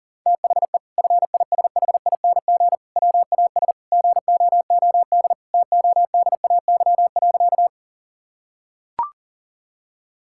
Morse code, heard as a keyed tone, about 40 wpm, 700 Hz. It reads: THE FISHING WAS GOOD TODAY.
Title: Sentences from Top 1000 Words - Review Part 5 - 40wpm